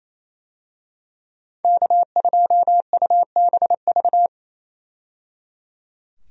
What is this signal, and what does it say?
Morse code, heard as a keyed tone, about 28 wpm, 700 Hz. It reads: K2UBV
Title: USA Call Signs - 5 Characters - 28wpm